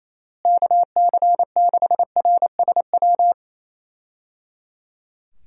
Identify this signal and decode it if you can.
Morse code, heard as a keyed tone, about 28 wpm, 700 Hz. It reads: KC6RSW
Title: USA Call Signs - 6 Characters - Repeated 3x - 28wpm